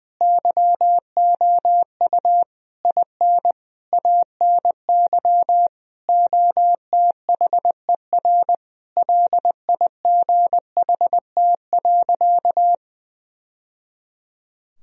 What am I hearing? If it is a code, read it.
Morse code, heard as a keyed tone, about 20 wpm, 700 Hz. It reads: YOU IN ANY OTHER LIGHT.
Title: Sentences from Top 1000 Words - Review Part 1 - 20wpm